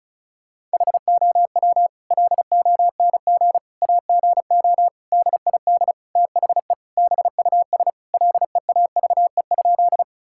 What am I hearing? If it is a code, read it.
Morse code, heard as a keyed tone, about 34 wpm, 700 Hz. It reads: HOW LONG AGO DID THE BUS LEAVE?
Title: Sentences from Top 1000 Words - Review Part 6 - 35wpm